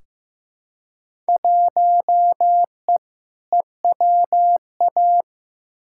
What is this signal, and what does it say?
Morse code, heard as a keyed tone, about 15 wpm, 700 Hz. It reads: 1E EWA
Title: Field Day Exchanges - 15wpm